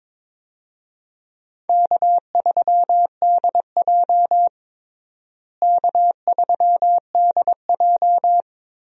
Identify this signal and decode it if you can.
Morse code, heard as a keyed tone, about 22 wpm, 700 Hz. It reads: K3DJ K3DJ
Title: USA Call Signs - 4 Characters - Repeated 2x - 22wpm